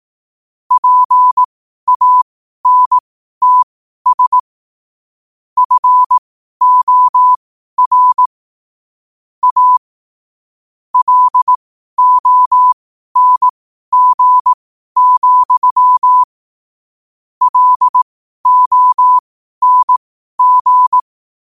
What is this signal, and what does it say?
Morse code, heard as a keyed tone, about 18 wpm, spaced as for 12 wpm, 1000 Hz. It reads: PANTS FOR A LONG, LONG
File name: qotdfiles / 12wpm